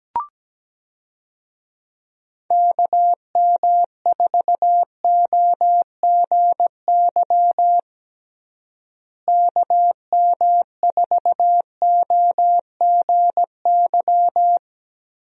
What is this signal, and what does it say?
Morse code, heard as a keyed tone, about 17 wpm, 700 Hz. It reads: KM4OGY KM4OGY
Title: USA Call Signs - 6 Characters - Repeated 3x - 17wpm